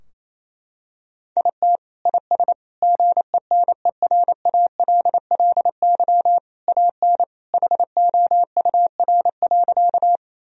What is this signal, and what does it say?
Morse code, heard as a keyed tone, about 28 wpm, 700 Hz. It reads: IT IS GENERALLY AN HOUR.
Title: Sentences from Top 1000 Words - Review Part 4 - 28wpm